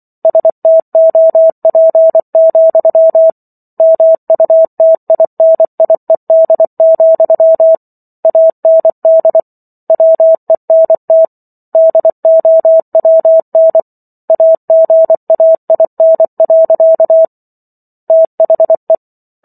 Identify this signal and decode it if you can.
Morse code, heard as a keyed tone, about 24 wpm, 650 Hz. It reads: STOP, MUTINIED, AND WENT DOWN AGAIN. THE
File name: Chapter 15 - 24 WPM